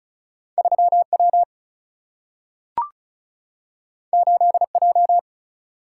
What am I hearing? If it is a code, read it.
Morse code, heard as a keyed tone, about 36 wpm, 700 Hz. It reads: 3 W 8 J
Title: Single-Letter Single-Number 35wpm